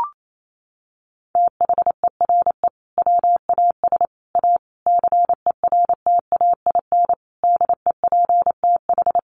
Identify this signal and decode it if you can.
Morse code, heard as a keyed tone, about 28 wpm, 700 Hz. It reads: THERE WAS A CERTAIN DEPTH
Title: Sentences from Top 1200 Words 28wpm